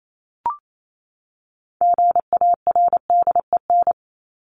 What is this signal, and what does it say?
Morse code, heard as a keyed tone, about 28 wpm, 700 Hz. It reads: GARDEN
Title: Top 501-750 Words 28wpm